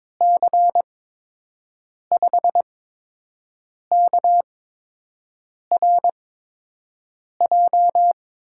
Morse code, heard as a keyed tone, about 21 wpm, spaced as for 6 wpm, 700 Hz. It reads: C5KRJ
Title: USA Call Signs - 6 Characters - 8x Character Spacing - 22wpm